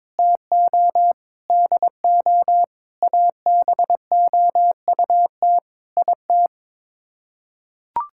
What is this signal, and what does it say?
Morse code, heard as a keyed tone, about 21 wpm, 700 Hz. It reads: TO DO ABOUT IT
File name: practice / 22wpm / Sets of 8 Words - 22wpm